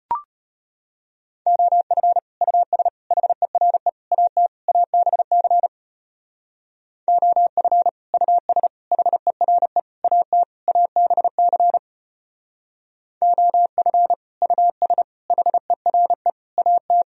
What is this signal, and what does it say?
Morse code, heard as a keyed tone, about 31 wpm, 700 Hz. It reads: OF US HERE AT ABC OF US HERE AT ABC OF US HERE AT
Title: Speed-Racing - Sets of 5 Words 25wpm